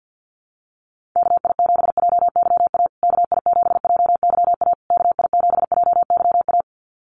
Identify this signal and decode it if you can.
Morse code, heard as a keyed tone, about 50 wpm, 700 Hz. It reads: KI7JYA KI7JYA KI7JYA
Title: USA Call Signs Repeated 3x 50wpm